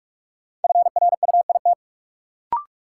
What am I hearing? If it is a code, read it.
Morse code, heard as a keyed tone, about 42 wpm, 700 Hz. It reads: FRUIT
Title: Sentences from Top 1300 Words - New Word Drill Alt - 45wpm